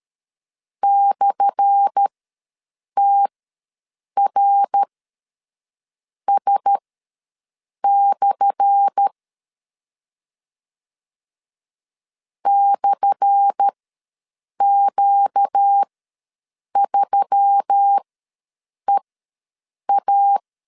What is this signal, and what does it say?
Morse code, heard as a keyed tone, about 13 wpm, spaced as for 6 wpm, 800 Hz. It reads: /TRS/ /Q3EA
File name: LEZIONI / 10 SIGNS / 10A SLASH ws 12wpm 3443